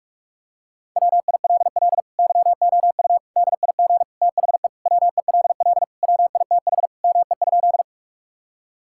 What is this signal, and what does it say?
Morse code, heard as a keyed tone, about 46 wpm, 700 Hz. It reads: WILL YOU DIG THE WELL WITH ME?
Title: Sentences from Top 1600 Words 45wpm